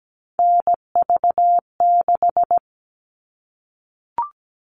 Morse code, heard as a keyed tone, about 17 wpm, 700 Hz. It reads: NV6
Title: All Cut Numbers - 3-Digit 17wpm